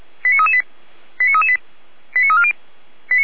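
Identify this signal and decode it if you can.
Five-tone selective call (ZVEI), heard as quick runs of five tones, tones of about 70 ms. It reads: 89338 89338 89340
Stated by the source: ZVEI five-tone call received on SDR, recorded by Feuerwehr Orlamünde.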